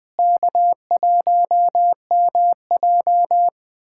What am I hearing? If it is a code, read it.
Morse code, heard as a keyed tone, about 20 wpm, 700 Hz. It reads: K1MJ
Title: USA Callsign DE Callsign 20wpm